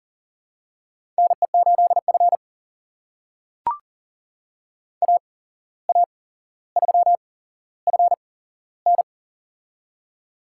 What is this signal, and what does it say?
Morse code, heard as a keyed tone, about 37 wpm, spaced as for 16 wpm, 700 Hz. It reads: NE8F AA3FN
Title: USA Call Signs - 8x Character Spacing - 40wpm